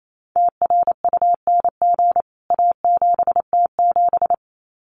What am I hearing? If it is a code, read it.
Morse code, heard as a keyed tone, about 28 wpm, 700 Hz. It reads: TRUNG A7T7
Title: CWT Exchange 28wpm